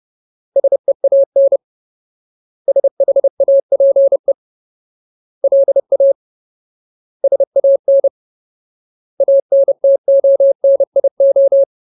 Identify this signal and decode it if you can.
Morse code, heard as a keyed tone, about 30 wpm, 550 Hz. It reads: SEAN SHAPE LA SAN ANTONIO
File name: Ultimate Word List / 30WPM Ultimate